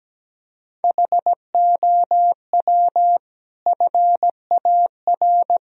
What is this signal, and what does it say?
Morse code, heard as a keyed tone, about 17 wpm, 700 Hz. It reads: HOW FAR
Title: Sentences from Top 1000 Words - Review Part 2 - 17wpm